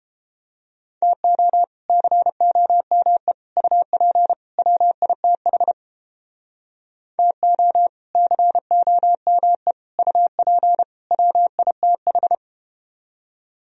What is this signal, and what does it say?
Morse code, heard as a keyed tone, about 32 wpm, 700 Hz. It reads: TO COME UP WITH TO COME UP WITH